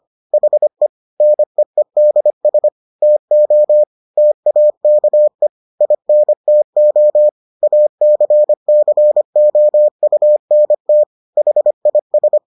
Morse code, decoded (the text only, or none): HE NEEDS TO TAKE INTO ACCOUNT HIS